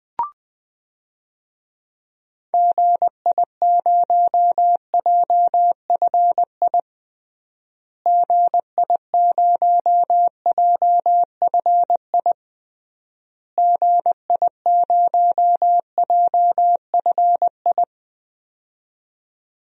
Morse code, heard as a keyed tone, about 21 wpm, 700 Hz. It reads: GI0JFI GI0JFI GI0JFI
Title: DXCC-100 Call Signs - 6 Characters - Repeated 3x - 20wpm